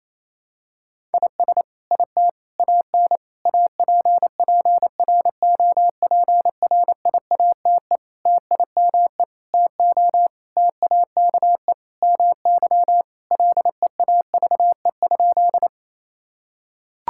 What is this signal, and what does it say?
Morse code, heard as a keyed tone, about 28 wpm, 700 Hz. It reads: IS IT AN APPROPRIATE TIME TO TAKE MY LEAVE?
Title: Sentences from Top 1000 Words - Review Part 5 - 28wpm